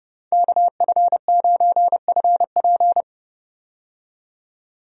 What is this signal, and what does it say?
Morse code, heard as a keyed tone, about 29 wpm, 700 Hz. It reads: KF9FP
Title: USA Call Signs - 5 Characters - 30wpm